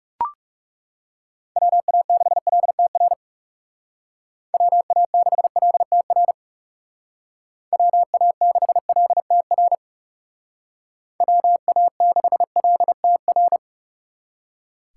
Morse code, heard as a keyed tone, about 36 wpm, 700 Hz. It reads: WA6LTR WA6LTR WA6LTR WA6LTR
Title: Speed Racing - USA Call Signs 30wpm